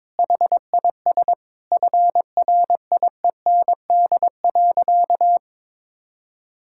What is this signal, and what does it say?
Morse code, heard as a keyed tone, about 22 wpm, 700 Hz. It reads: HIS FRIEND.
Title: Sentences from Top 1000 Words - Review Part 5 - 22wpm